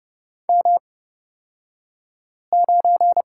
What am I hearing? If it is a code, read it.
Morse code, heard as a keyed tone, about 30 wpm, spaced as for 5 wpm, 700 Hz. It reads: M9
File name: Lesson 37d - Letter x - Two Characters 30wpm